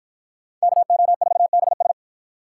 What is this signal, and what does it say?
Morse code, heard as a keyed tone, about 51 wpm, 700 Hz. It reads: KO4BS